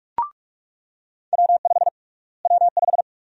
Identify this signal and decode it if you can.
Morse code, heard as a keyed tone, about 46 wpm, 690 Hz. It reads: W5 W5
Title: USA Call Sign Prefixes - Repeated 2x - 45wpm